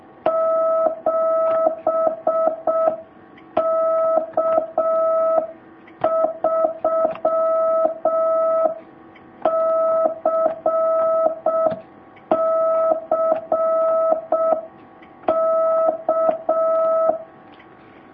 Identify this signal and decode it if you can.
Morse code, heard as a keyed tone, about 6 wpm, 650 Hz. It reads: 7K3CCK
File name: jh8cbh / 2019cw